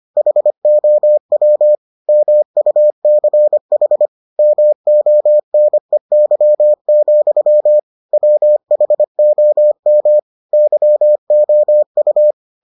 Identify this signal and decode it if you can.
Morse code, heard as a keyed tone, about 25 wpm, 600 Hz. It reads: HOW MUCH MONEY, WHOM YOU